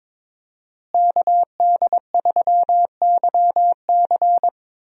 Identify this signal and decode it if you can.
Morse code, heard as a keyed tone, about 22 wpm, 700 Hz. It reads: KD3YC
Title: USA Call Signs - 5 Characters - Repeated 3x - 22wpm